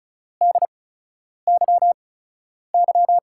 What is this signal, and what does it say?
Morse code, heard as a keyed tone, about 36 wpm, spaced as for 10 wpm, 700 Hz. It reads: DYY
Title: USA Call Signs - 5 Characters - 8x Character Spacing - 35wpm